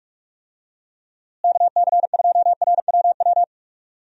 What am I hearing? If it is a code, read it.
Morse code, heard as a keyed tone, about 41 wpm, 700 Hz. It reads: KC2RWW
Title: USA Call Signs - 6 Characters - Repeated 2x - 45wpm